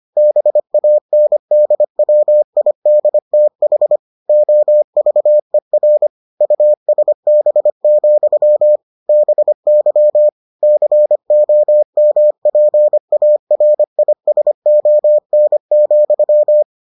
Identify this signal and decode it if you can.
Morse code, heard as a keyed tone, about 25 wpm, 600 Hz. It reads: BANDWIDTH OVER USB, BY COMPARISON,